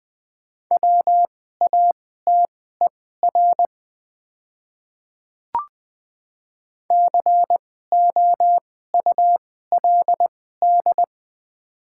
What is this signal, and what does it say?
Morse code, heard as a keyed tone, about 20 wpm, spaced as for 14 wpm, 700 Hz. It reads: WATER COULD
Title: 2x Character Spacing - Top 100 Words - 20wpm